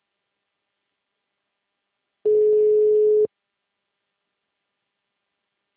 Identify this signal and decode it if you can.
Ringback tone (Europe, 425 Hz).